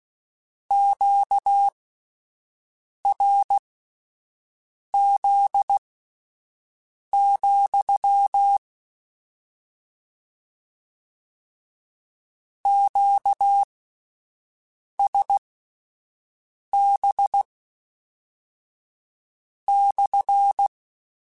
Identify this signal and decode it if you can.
Morse code, heard as a keyed tone, about 16 wpm, spaced as for 5 wpm, 800 Hz. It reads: QRZ, QSB /
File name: Q Calls / Q Calls-mp3s / Q CALLS SPSL @10